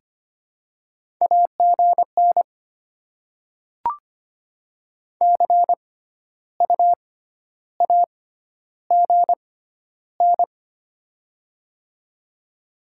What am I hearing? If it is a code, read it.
Morse code, heard as a keyed tone, about 25 wpm, spaced as for 12 wpm, 700 Hz. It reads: AGN CU AGN
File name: Top 20 QSO Elements - 6x Character Spacing - 25wpm